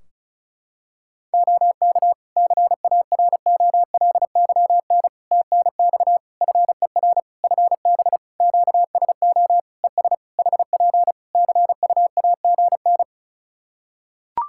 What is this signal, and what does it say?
Morse code, heard as a keyed tone, about 36 wpm, 700 Hz. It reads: OK CAROLYN TNX FER FB QSO ES HP CUAGN
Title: Practice QSO - Exchange 4 - Ending - 35wpm